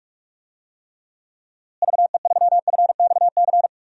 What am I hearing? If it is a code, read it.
Morse code, heard as a keyed tone, about 46 wpm, 700 Hz. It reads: VE3FXC
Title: DXCC-100 Call Signs - 6 Characters - Repeated 3x - 45wpm